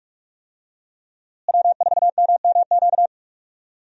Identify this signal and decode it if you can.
Morse code, heard as a keyed tone, about 42 wpm, 700 Hz. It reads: W4MMQ
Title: USA Call Signs - 5 Characters - Repeated 3x - 45wpm